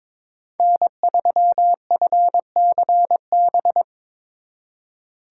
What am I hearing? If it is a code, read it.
Morse code, heard as a keyed tone, about 25 wpm, 700 Hz. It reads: N3FCB
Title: Speed Racing - USA 1x3 Call Signs 22wpm